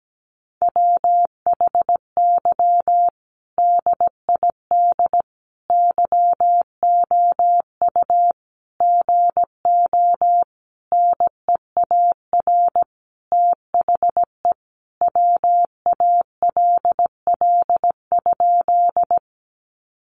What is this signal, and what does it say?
Morse code, heard as a keyed tone, about 17 wpm, 700 Hz. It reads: WHY DID YOU GO NEAR THE WALL?
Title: Sentences from Top 1100 Words 17wpm